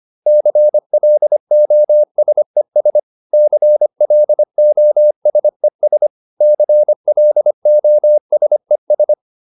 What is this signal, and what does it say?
Morse code, heard as a keyed tone, about 25 wpm, 600 Hz. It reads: CLOSES CLOSES CLOSES